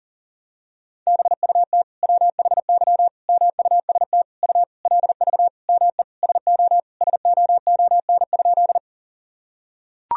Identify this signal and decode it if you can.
Morse code, heard as a keyed tone, about 40 wpm, 700 Hz. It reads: BUT WHY MUST U LV ME SO SOON?
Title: Sentences from Top 400 Words and Abbreviations 40wpm